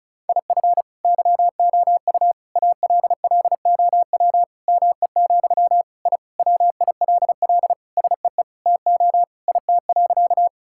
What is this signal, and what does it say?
Morse code, heard as a keyed tone, about 36 wpm, 700 Hz. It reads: IF YOU ALLOW ME, I WILL SEE TO IT.
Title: Sentences from Top 1000 Words - Review Part 5 - 35wpm